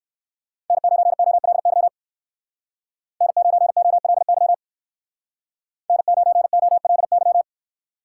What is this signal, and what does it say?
Morse code, heard as a keyed tone, about 63 wpm, 700 Hz. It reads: N9OLY N9OLY N9OLY